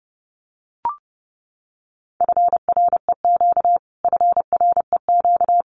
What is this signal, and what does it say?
Morse code, heard as a keyed tone, about 28 wpm, 700 Hz. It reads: FREQ FREQ
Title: Abbreviations Repeated 3x - 30wpm